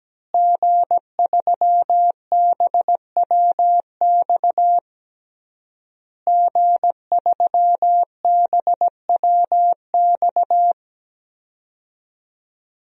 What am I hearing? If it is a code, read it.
Morse code, heard as a keyed tone, about 17 wpm, 700 Hz. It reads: G3BWX G3BWX
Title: DXCC-100 Call Signs - 5 Characters - Repeated 3x - 17wpm